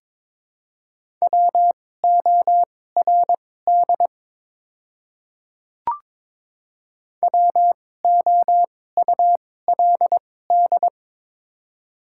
Morse code, heard as a keyed tone, about 22 wpm, spaced as for 16 wpm, 700 Hz. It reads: WORD WOULD
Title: 2x Character Spacing - Top 100 Words - 22wpm